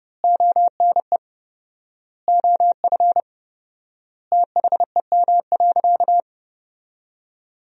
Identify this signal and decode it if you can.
Morse code, heard as a keyed tone, about 30 wpm, 700 Hz. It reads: ONE OF THEM.
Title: 4x Word-Spacing - Sentences from Top 300 Words - 30wpm